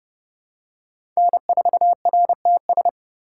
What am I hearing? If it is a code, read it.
Morse code, heard as a keyed tone, about 30 wpm, 700 Hz. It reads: N4RTS
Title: USA Call Signs - 5 Characters - 4x Character Spacing - 30wpm